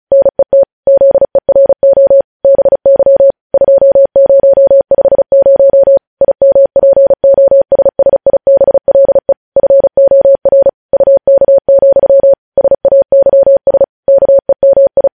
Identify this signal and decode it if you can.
Morse code, heard as a keyed tone, about 36 wpm, 550 Hz. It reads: NET ZERO BY 2050 IMPOSSIBLE FOR UK, SAYS KEMI
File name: podcast / News headlines-1x-35x35